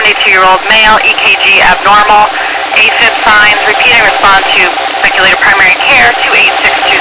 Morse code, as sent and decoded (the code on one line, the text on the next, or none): none